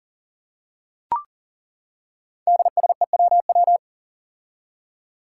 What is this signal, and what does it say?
Morse code, heard as a keyed tone, about 40 wpm, 700 Hz. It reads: DSEWW
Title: Five Challenging Letter-Number Combinations 40wpm